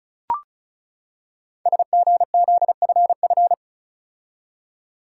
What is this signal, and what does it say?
Morse code, heard as a keyed tone, about 36 wpm, 690 Hz. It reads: SGZFF
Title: Code Groups - 5 Letters - 35wpm